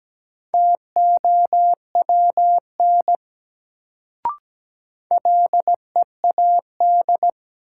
Morse code, heard as a keyed tone, about 17 wpm, 700 Hz. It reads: TOWN LEAD
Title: Sentences from Top 500 Words - New Word Drill - 17wpm